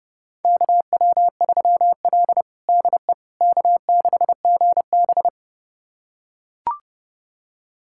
Morse code, heard as a keyed tone, about 30 wpm, 700 Hz. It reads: KW3L DE K6GB
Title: USA Callsign DE Callsign - 4 Characters - 30wpm